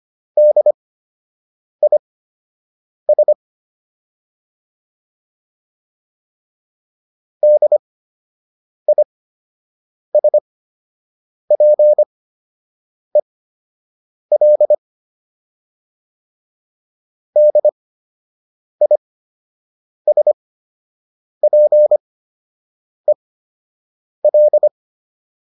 Morse code, heard as a keyed tone, about 25 wpm, spaced as for 7 wpm, 600 Hz. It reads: DIS DISPEL DISPEL